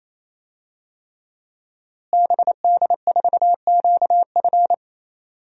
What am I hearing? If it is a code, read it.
Morse code, heard as a keyed tone, about 28 wpm, 700 Hz. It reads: BD4QF
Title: DXCC-100 Call Signs - 5 Characters - Repeated 3x - 28wpm